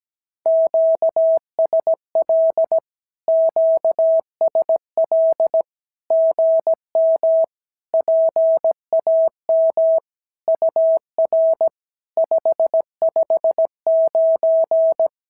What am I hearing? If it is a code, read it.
Morse code, heard as a keyed tone, about 17 wpm, 650 Hz. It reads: QSL QSL GM PAM UR 559
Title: SOTA QSO Practice 17wpm